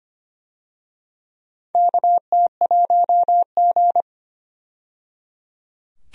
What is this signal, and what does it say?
Morse code, heard as a keyed tone, about 25 wpm, 700 Hz. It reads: KT1G
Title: USA Call Signs - 4 Characters - 25wpm